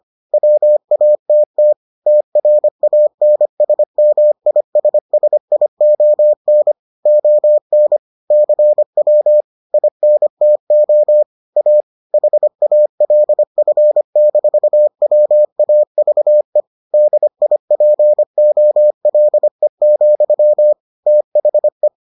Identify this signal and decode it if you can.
Morse code, heard as a keyed tone, about 25 wpm, 600 Hz. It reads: WATT TRANSMISSION ON CW INTO A HALF-WAVE DIPOLE, THE